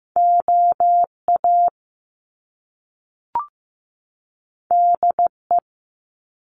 Morse code, heard as a keyed tone, about 15 wpm, 700 Hz. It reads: OA DE
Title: Top Two Letter Combinations in Words 15wpm